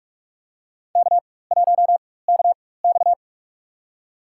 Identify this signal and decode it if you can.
Morse code, heard as a keyed tone, about 46 wpm, spaced as for 21 wpm, 700 Hz. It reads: K1KX
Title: USA Call Signs - 4 Characters - 4x Character Spacing - 45wpm